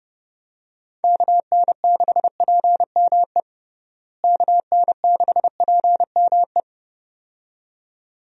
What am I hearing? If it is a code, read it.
Morse code, heard as a keyed tone, about 30 wpm, 700 Hz. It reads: KN6PME KN6PME
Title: USA Call Signs - 6 Characters - Repeated 2x - 30wpm